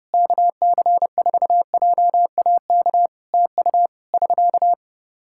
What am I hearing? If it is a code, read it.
Morse code, heard as a keyed tone, about 28 wpm, 700 Hz. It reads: KC4JAK TU <SK>
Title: Practice QSO - Exchange 4 - Ending - 30wpm